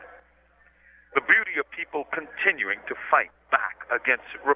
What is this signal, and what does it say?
Emergency Alert System (SAME) tones, heard as harsh buzzing bursts, with speech. It is heard from 1.2 to 4.1 s.